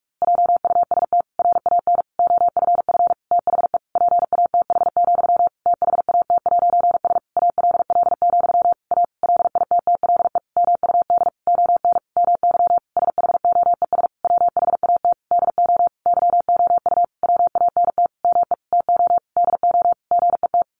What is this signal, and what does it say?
Morse code, heard as a keyed tone, about 44 wpm, 700 Hz. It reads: JUST RAN OFF THE PATH, THAT'S ALL, A LITTLE MUD ON MY SHOES WHAT DO YOU WANT ME TO DO GET
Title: A Sound Of Thunder 45wpm